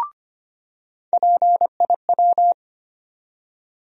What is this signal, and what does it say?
Morse code, heard as a keyed tone, about 25 wpm, 700 Hz. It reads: PIW
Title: USA Call Sign Postfixes - 3 Letters 25wpm